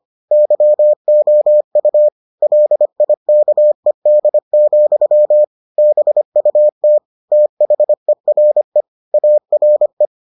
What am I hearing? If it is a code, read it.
Morse code, heard as a keyed tone, about 25 wpm, 600 Hz. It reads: YOU LIKED, BUT THERE ARE